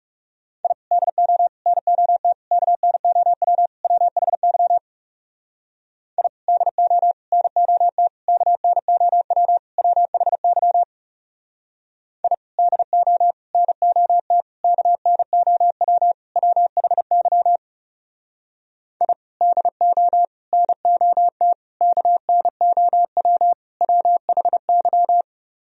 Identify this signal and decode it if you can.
Morse code, heard as a keyed tone, about 38 wpm, 700 Hz. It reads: I DO NOT KNOW WHY I DO NOT KNOW WHY I DO NOT KNOW WHY I DO NOT KNOW WHY